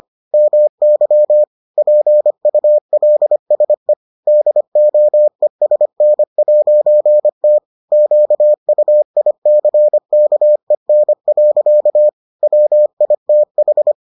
Morse code, heard as a keyed tone, about 25 wpm, 600 Hz. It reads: MY PULSE DOESN'T QUICKEN. WITH